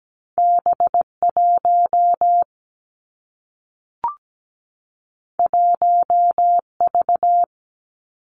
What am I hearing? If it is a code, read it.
Morse code, heard as a keyed tone, about 17 wpm, 700 Hz. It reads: B1 1V
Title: Two Challenging Letter-Number Combinations 17wpm